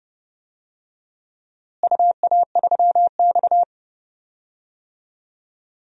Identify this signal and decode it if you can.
Morse code, heard as a keyed tone, about 31 wpm, 700 Hz. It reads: UA3X
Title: DXCC-100 Call Signs - 4 Characters - Repeated 2x - 30wpm